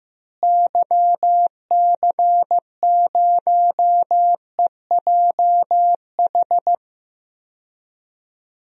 Morse code, heard as a keyed tone, about 15 wpm, 700 Hz. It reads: YC0EJH
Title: DXCC-100 Call Signs - 6 Characters - Repeated 3x - 15wpm